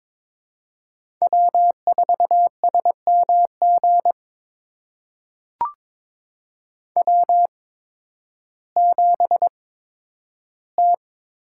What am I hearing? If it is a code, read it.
Morse code, heard as a keyed tone, about 22 wpm, spaced as for 12 wpm, 700 Hz. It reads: W4SMG W7T
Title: USA Call Signs - 5 Characters - 8x Character Spacing - 22wpm